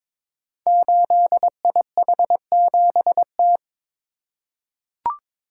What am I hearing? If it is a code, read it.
Morse code, heard as a keyed tone, about 23 wpm, 700 Hz. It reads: 8IH7T
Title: Code Groups - 5 Letter:Numbers - 22wpm